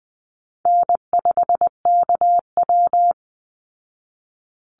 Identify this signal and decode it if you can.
Morse code, heard as a keyed tone, about 20 wpm, 700 Hz. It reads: N5KW
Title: Easy Call Signs Repeated 2x - 20wpm